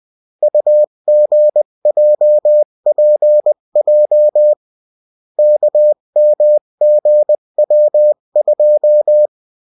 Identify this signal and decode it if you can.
Morse code, heard as a keyed tone, about 20 wpm, 600 Hz. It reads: UGJPJ KMGW2
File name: morse mp3 / revision9-vitesse 18 mots